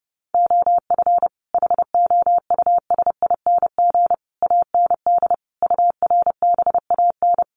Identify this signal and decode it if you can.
Morse code, heard as a keyed tone, about 30 wpm, 700 Hz. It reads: OF HOUSING AND URBAN
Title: Sets of 5 Words Encore2 30wpm